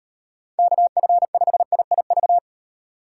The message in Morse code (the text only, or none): KF5IIV